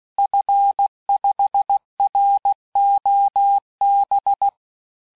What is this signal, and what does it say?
Morse code, heard as a keyed tone, about 16 wpm, 800 Hz. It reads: F5ROB
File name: cours-du-lundi files / cours-1580-fin-QSO